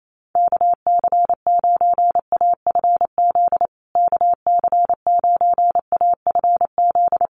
Morse code, heard as a keyed tone, about 28 wpm, 700 Hz. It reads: KC9AFZ KC9AFZ
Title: USA Call Signs Repeated 3x 28wpm